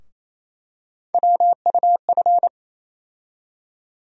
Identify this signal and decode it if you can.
Morse code, heard as a keyed tone, about 28 wpm, 700 Hz. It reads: WUF
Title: USA Call Sign Postfixes - 3 Letters 28wpm